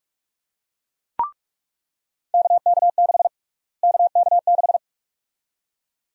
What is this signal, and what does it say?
Morse code, heard as a keyed tone, about 42 wpm, 700 Hz. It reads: KK6 KK6
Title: USA Call Sign Prefixes - Repeated 2x - 45wpm